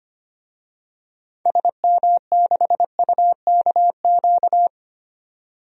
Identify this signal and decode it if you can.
Morse code, heard as a keyed tone, about 25 wpm, 700 Hz. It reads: SM6UKQ
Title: DXCC-100 Call Signs - 6 Characters - 25wpm